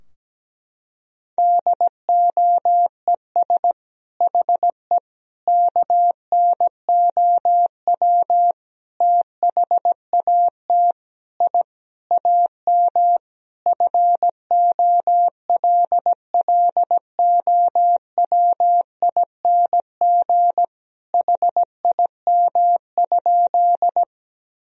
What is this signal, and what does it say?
Morse code, heard as a keyed tone, about 17 wpm, 700 Hz. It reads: DOES HE KNOW THAT I AM FOLLOWING HIM?
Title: Sentences from Top 1000 Words - Review Part 4 - 17wpm